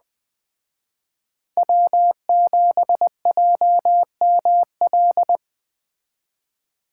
Morse code, heard as a keyed tone, about 20 wpm, 700 Hz. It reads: W7JML
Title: Speed Racing - USA 1x3 Call Signs 17wpm